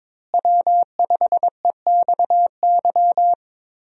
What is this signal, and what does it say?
Morse code, heard as a keyed tone, about 22 wpm, 700 Hz. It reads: W5EXY
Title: USA Call Signs - 5 Characters - Repeated 2x - 22wpm